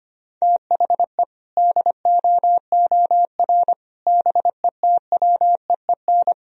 Morse code, heard as a keyed tone, about 25 wpm, 700 Hz. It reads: THE DOOR BETWEEN
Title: Sentences from Top 1000 Words - Review Part 1 - 25wpm